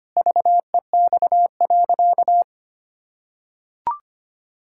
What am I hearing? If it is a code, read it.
Morse code, heard as a keyed tone, about 25 wpm, 700 Hz. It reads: VEX.
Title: Pangram Challenge v2 - 25wpm